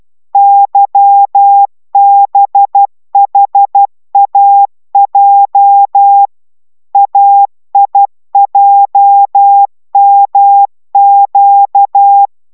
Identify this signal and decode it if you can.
Morse code, heard as a keyed tone, about 12 wpm, 800 Hz. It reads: YBHAJ AIJMQ